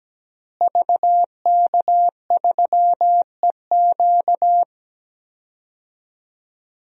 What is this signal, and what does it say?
Morse code, heard as a keyed tone, about 17 wpm, 700 Hz. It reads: VK3EQ
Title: DXCC-100 Call Signs - 5 Characters - Repeated 3x - 17wpm